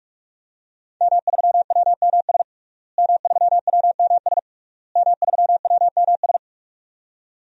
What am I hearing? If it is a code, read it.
Morse code, heard as a keyed tone, about 47 wpm, 700 Hz. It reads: M3WMS M3WMS M3WMS
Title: DXCC-100 Call Signs - 5 Characters - Repeated 3x - 45wpm